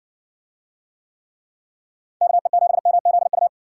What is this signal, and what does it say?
Morse code, heard as a keyed tone, about 60 wpm, 700 Hz. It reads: KE7MZU